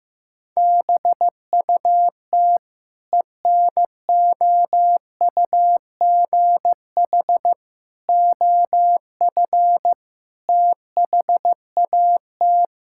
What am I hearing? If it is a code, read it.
Morse code, heard as a keyed tone, about 15 wpm, 700 Hz. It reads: BUT ENOUGH OF THAT
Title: Sentences from Top 1000 Words - Review Part 5 - 15wpm